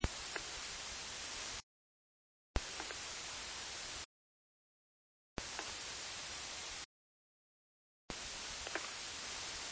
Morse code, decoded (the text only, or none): none